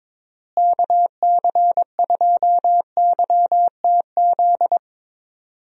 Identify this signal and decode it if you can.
Morse code, heard as a keyed tone, about 22 wpm, 700 Hz. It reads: KC2YTZ
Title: USA Call Signs - 6 Characters - Repeated 2x - 22wpm